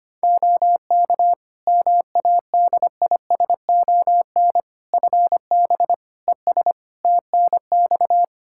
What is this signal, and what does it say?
Morse code, heard as a keyed tone, about 25 wpm, 700 Hz. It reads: OK MADISON FB ES TNX
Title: Practice QSO - Exchange 2 - Info - 25wpm